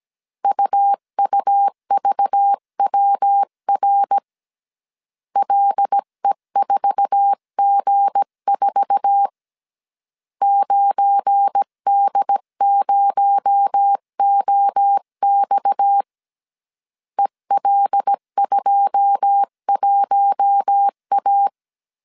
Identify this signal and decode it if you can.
Morse code, heard as a keyed tone, about 17 wpm, 800 Hz. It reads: UUVWR LE4G4 9D0OX EL21A
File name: CW / DET / 1 9 1500C S B 16wpm 2856